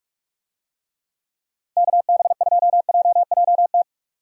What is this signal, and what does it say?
Morse code, heard as a keyed tone, about 46 wpm, 700 Hz. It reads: KB2JJT
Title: USA Call Signs - 6 Characters - 45wpm